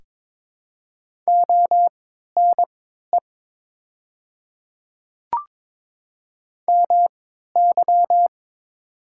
Morse code, heard as a keyed tone, about 22 wpm, spaced as for 12 wpm, 700 Hz. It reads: ONE MY
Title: 3x Character Spacing - Top 100 Words - 22wpm